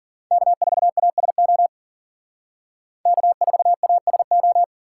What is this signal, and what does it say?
Morse code, heard as a keyed tone, about 51 wpm, 700 Hz. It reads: K4ASO K4ASO